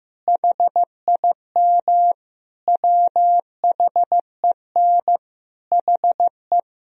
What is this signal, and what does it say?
Morse code, heard as a keyed tone, about 16 wpm, 700 Hz. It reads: HIM WHEN HE
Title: Sentences from Top 1000 Words - Review Part 2 - 15wpm